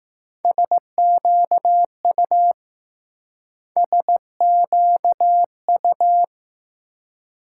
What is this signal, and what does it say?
Morse code, heard as a keyed tone, about 20 wpm, 700 Hz. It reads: SQU SQU